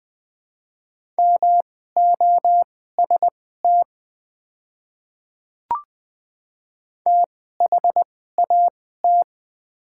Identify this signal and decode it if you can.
Morse code, heard as a keyed tone, about 20 wpm, spaced as for 14 wpm, 700 Hz. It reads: MOST THAT
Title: 2x Character Spacing - Top 100 Words - 20wpm